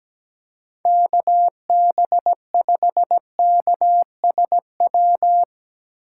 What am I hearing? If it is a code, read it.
Morse code, heard as a keyed tone, about 17 wpm, 700 Hz. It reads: KB5KSW
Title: USA Call Signs - 6 Characters - 4x Character Spacing - 17wpm